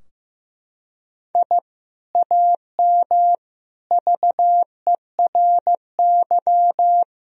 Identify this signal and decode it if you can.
Morse code, heard as a keyed tone, about 15 wpm, 700 Hz. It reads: I AM VERY
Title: Sentences from Top 1000 Words - Review Part 2 - 15wpm